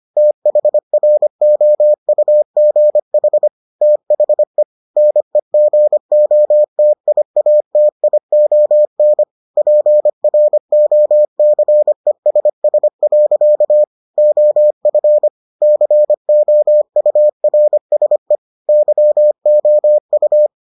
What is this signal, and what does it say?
Morse code, heard as a keyed tone, about 25 wpm, 600 Hz. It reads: THROUGH THE NEGOTIATION PROCESS. OF COURSE YOU